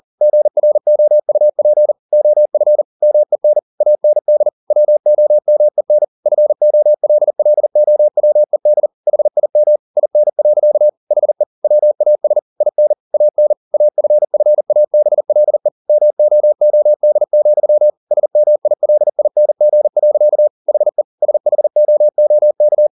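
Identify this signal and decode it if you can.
Morse code, heard as a keyed tone, about 40 wpm, 600 Hz. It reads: GROUP OF MEN AND WOMEN FOLLOWED HIM IN. HE WAS IN AN AFFABLE MOOD, SMILING. HE SHOOK
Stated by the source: Morse code at 40 wpm, practice text, CW med Gnister.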